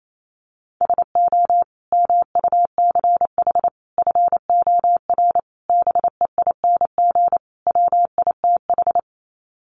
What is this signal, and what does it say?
Morse code, heard as a keyed tone, about 28 wpm, 690 Hz. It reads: SO MUCH FOR BEING WITH
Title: Sets of 5 Words Encore 28wpm